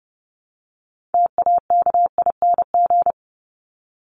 Morse code, heard as a keyed tone, about 28 wpm, 700 Hz. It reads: TAKING
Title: ING Suffix - Top 100 Words with Base Words 30wpm